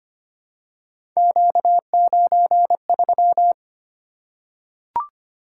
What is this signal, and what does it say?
Morse code, heard as a keyed tone, about 25 wpm, 700 Hz. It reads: Q93
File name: Three Letter-Number Combinations - 25wpm